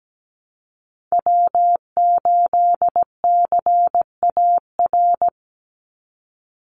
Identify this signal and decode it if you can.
Morse code, heard as a keyed tone, about 17 wpm, 700 Hz. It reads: W8CAR
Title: Call Signs Repeated 3x - 17wpm